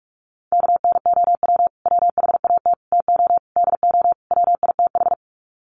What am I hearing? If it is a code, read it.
Morse code, heard as a keyed tone, about 44 wpm, 700 Hz. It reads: KNOW WHAT TO DO WITH
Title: Sets of 5 Words 45wpm